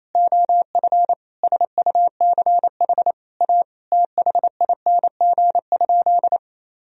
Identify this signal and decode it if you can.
Morse code, heard as a keyed tone, about 28 wpm, 700 Hz. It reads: OF SUCH A THING?
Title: Sentences from Top 200 Words and Abbreviations 28wpm